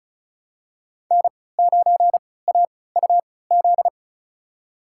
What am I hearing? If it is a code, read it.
Morse code, heard as a keyed tone, about 36 wpm, spaced as for 20 wpm, 700 Hz. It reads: N9AUZ
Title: USA Call Signs - 5 Characters - 3x Character Spacing - 35wpm